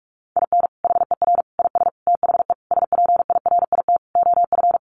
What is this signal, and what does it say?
Morse code, heard as a keyed tone, about 44 wpm, 700 Hz. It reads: IN HER IS THE SPIRIT OF
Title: Sentences from Top 1400 Words 45wpm